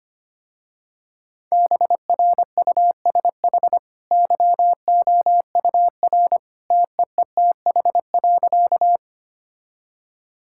Morse code, heard as a keyed tone, about 25 wpm, 700 Hz. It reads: BRUSH YOUR TEETH.
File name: Easy Sentences with Hints - 25wpm